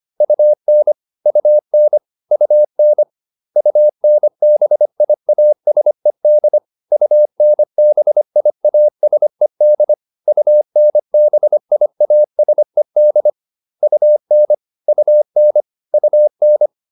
Morse code, heard as a keyed tone, about 25 wpm, 600 Hz. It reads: UN UN UN UNBIASED UNBIASED UNBIASED UN UN UN